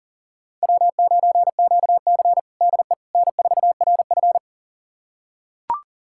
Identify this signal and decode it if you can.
Morse code, heard as a keyed tone, about 40 wpm, 700 Hz. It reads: W9QC DE N4RF
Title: USA Callsign DE Callsign - 4 Characters - 40wpm